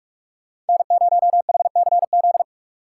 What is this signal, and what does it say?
Morse code, heard as a keyed tone, about 59 wpm, 700 Hz. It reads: N0HCZ